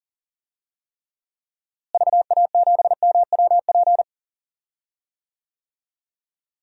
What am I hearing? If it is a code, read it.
Morse code, heard as a keyed tone, about 40 wpm, 700 Hz. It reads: VA7MWP
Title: DXCC-100 Call Signs - 6 Characters - Repeated 2x - 40wpm